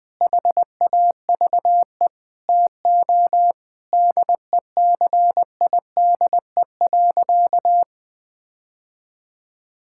Morse code, heard as a keyed tone, about 20 wpm, 700 Hz. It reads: HAVE TO DECIDE.
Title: Sentences with 5-Word Phrases - Spoken Beginning - 20wpm